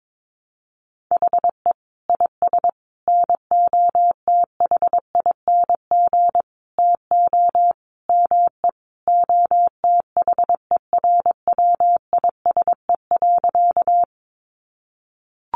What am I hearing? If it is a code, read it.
Morse code, heard as a keyed tone, about 22 wpm, 700 Hz. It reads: HE IS NOTHING TO ME OTHERWISE.
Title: Sentences from Top 1200 Words 22wpm